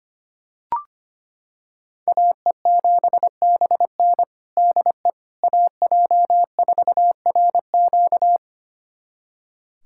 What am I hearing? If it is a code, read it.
Morse code, heard as a keyed tone, about 25 wpm, 700 Hz. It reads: AE7BN DE AJ4RQ
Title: USA Callsign DE Callsign - 5 Characters - 25wpm